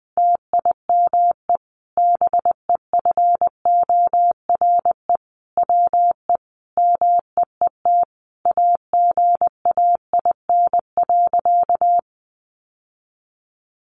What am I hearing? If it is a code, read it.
Morse code, heard as a keyed tone, about 19 wpm, 700 Hz. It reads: TIME BEFORE WE MEET AGAIN.
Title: Sentences from Top 1100 Words 20wpm